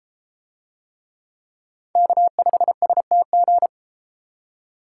Morse code, heard as a keyed tone, about 32 wpm, 700 Hz. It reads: K5STG